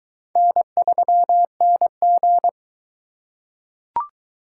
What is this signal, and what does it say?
Morse code, heard as a keyed tone, about 23 wpm, 700 Hz. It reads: N3NG